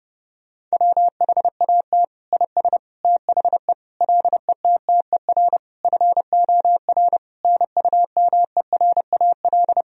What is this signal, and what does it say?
Morse code, heard as a keyed tone, about 30 wpm, 700 Hz. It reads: WHAT IS THE LETTER FOR NUMERAL
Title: Sentences from Top 1000 Words - Review Part 2 - 30wpm